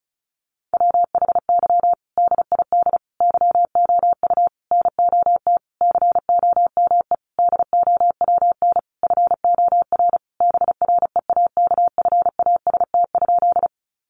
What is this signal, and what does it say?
Morse code, heard as a keyed tone, about 34 wpm, 700 Hz. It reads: WHY DID YOU NOT COME DOWN FOR BREAKFAST?
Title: Sentences from Top 1500 Words 35wpm